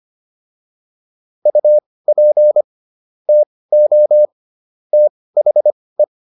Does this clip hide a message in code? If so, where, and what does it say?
Yes: Morse code from 1.5 to 6.0 s, heard as a keyed tone, about 25 wpm, spaced as for 18 wpm, 600 Hz. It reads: UP TO THE